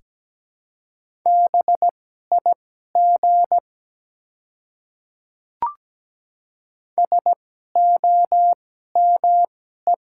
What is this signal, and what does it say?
Morse code, heard as a keyed tone, about 17 wpm, spaced as for 12 wpm, 700 Hz. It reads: BIG SOME
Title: 2x Character Spacing - Top 200 Words - 17wpm